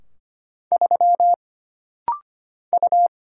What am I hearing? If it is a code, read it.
Morse code, heard as a keyed tone, about 25 wpm, 700 Hz. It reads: 3 U
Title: Single Letter-Number Alt - Rapid-Fire 25wpm